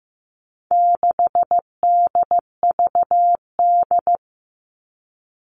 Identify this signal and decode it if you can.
Morse code, heard as a keyed tone, about 15 wpm, 700 Hz. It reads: 6DVD
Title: All Cut Numbers - 4-Digit 15wpm